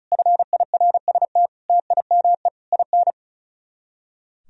Morse code, heard as a keyed tone, about 39 wpm, 700 Hz. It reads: FIRST TIME IN
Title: Speed-Racing - Sets of 4 Words 35wpm